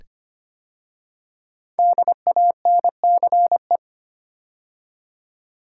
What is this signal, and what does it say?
Morse code, heard as a keyed tone, about 25 wpm, 700 Hz. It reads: DANCE
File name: 5 Letter Words - First 3 Letters Given - 25wpm